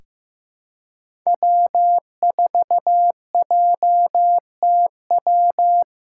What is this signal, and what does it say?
Morse code, heard as a keyed tone, about 15 wpm, 700 Hz. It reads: W4JTW
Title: USA Call Signs - 5 Characters - 2x Character Spacing - 15wpm